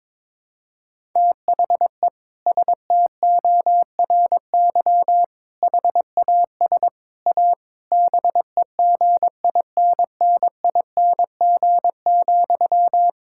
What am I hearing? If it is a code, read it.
Morse code, heard as a keyed tone, about 22 wpm, 700 Hz. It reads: THE STORY HAS A BEGINNING,
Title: Sentences from Top 1000 Words - Review Part 6 - 22wpm